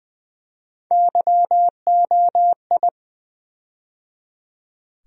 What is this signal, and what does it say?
Morse code, heard as a keyed tone, about 20 wpm, 700 Hz. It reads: YOI
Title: USA Call Sign Postfixes - 3 Letters 20wpm